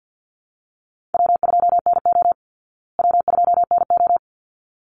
Morse code, heard as a keyed tone, about 49 wpm, 700 Hz. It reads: W2NO W2NO
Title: Call Signs Repeated 3x - 50wpm